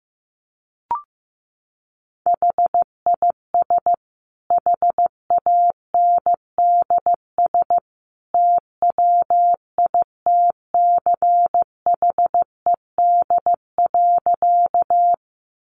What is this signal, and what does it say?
Morse code, heard as a keyed tone, about 15 wpm, 700 Hz. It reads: HIS HANDS TWITCHED.
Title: A Sound Of Thunder 15wpm